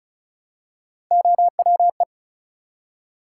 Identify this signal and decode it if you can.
Morse code, heard as a keyed tone, about 36 wpm, 700 Hz. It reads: OWE
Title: Three Letter Words - 35wpm